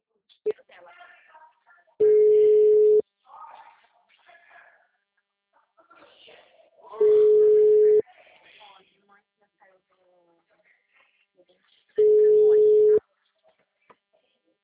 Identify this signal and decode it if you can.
Ringback tone (Europe, 425 Hz), with speech.